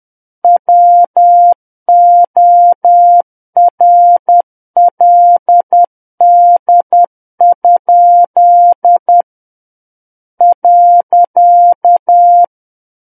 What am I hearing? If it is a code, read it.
Morse code, heard as a keyed tone, about 10 wpm, 700 Hz. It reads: WORLD? .
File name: Book One / 01-Chapter 1 - 10 WPM